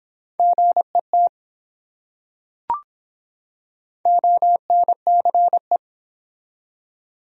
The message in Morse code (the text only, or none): GET ONCE